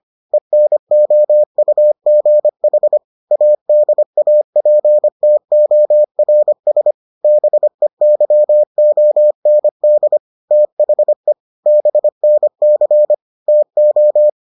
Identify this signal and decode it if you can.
Morse code, heard as a keyed tone, about 25 wpm, 600 Hz. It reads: ENOUGH ADAPTORS BEYOND THE BNC TO